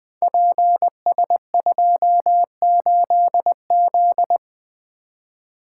Morse code, heard as a keyed tone, about 20 wpm, 700 Hz. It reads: PS28Z
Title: Code Groups - 5 Letter:Numbers - 20wpm